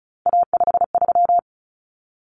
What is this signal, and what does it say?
Morse code, heard as a keyed tone, about 34 wpm, 700 Hz. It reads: A53
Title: Common Cut Numbers - 3-Digit 35wpm